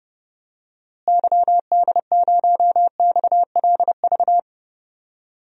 Morse code, heard as a keyed tone, about 30 wpm, 700 Hz. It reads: YD0XLV
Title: DXCC-100 Call Signs - 6 Characters - Repeated 3x - 30wpm